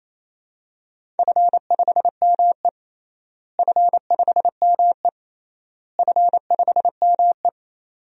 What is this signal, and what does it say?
Morse code, heard as a keyed tone, about 28 wpm, 700 Hz. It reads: F5ME F5ME F5ME
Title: DXCC-100 Call Signs - 4 Characters - Repeated 3x - 28wpm